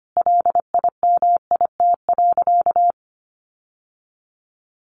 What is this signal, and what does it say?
Morse code, heard as a keyed tone, about 25 wpm, 700 Hz. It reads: LIMIT.
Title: Sentences from Top 1500 Words 25wpm